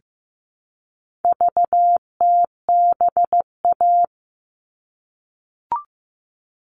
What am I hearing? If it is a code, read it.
Morse code, heard as a keyed tone, about 15 wpm, 700 Hz. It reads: VTBA
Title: All Cut Numbers - 4-Digit 15wpm